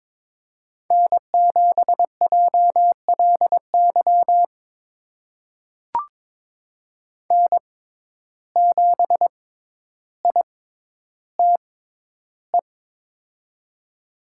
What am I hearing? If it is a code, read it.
Morse code, heard as a keyed tone, about 22 wpm, spaced as for 11 wpm, 700 Hz. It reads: N7JLY N7ITE
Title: USA Call Signs - 5 Characters - 6x Character Spacing - 22wpm